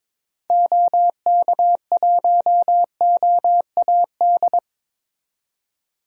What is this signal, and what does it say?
Morse code, heard as a keyed tone, about 21 wpm, 700 Hz. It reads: OK1OAD
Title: DXCC-100 Call Signs - 6 Characters - 22wpm